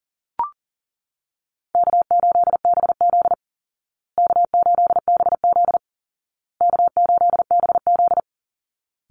Morse code, heard as a keyed tone, about 40 wpm, 700 Hz. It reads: K8BZ K8BZ K8BZ
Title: Easy Call Signs Repeated 3x - 40wpm